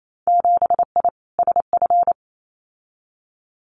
Morse code, heard as a keyed tone, about 28 wpm, 700 Hz. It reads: 7I SF
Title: Winter Field Day Exchanges Encore 28wpm